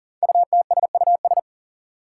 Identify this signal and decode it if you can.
Morse code, heard as a keyed tone, about 42 wpm, 700 Hz. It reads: UTSUS
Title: Five Challenging Letter-Number Combinations 40wpm